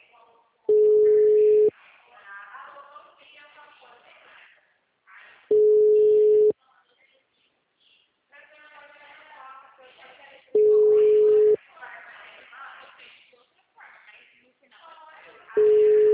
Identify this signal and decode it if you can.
Ringback tone (Europe, 425 Hz), with speech.
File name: out- -2039- - - .29401